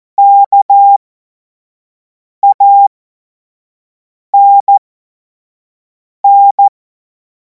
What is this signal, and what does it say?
Morse code, heard as a keyed tone, about 14 wpm, spaced as for 5 wpm, 800 Hz. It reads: KANN